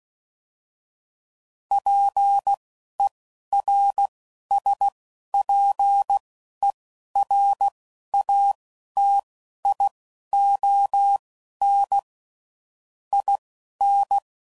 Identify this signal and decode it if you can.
Morse code, heard as a keyed tone, about 16 wpm, spaced as for 11 wpm, 800 Hz. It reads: PERSPERATION IN
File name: Sayings / Quotes / Persperation